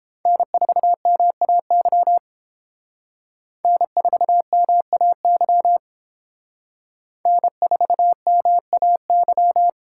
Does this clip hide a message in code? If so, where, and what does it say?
Yes: Morse code from 0.3 to 9.7 s, heard as a keyed tone, about 31 wpm, 700 Hz. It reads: N4MAY N4MAY N4MAY